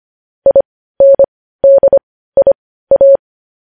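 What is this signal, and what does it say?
Morse code, heard as a keyed tone, about 25 wpm, spaced as for 15 wpm, 550 Hz. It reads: INDIA
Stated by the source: Morse news headlines sent at 15 words per minute.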